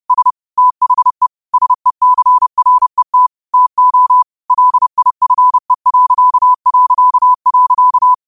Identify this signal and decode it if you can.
Morse code, heard as a keyed tone, about 30 wpm, 1000 Hz. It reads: S THE SECRET TO LIFE...
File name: qotdfiles / 30wpm